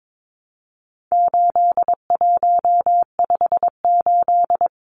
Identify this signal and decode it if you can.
Morse code, heard as a keyed tone, about 22 wpm, 700 Hz. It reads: 8158
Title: Numbers - 4-Digit 22wpm